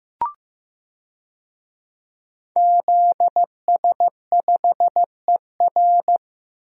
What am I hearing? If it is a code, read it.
Morse code, heard as a keyed tone, about 15 wpm, 700 Hz. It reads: ZS5ER
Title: DXCC-100 Call Signs - 5 Characters - Repeated 3x - 15wpm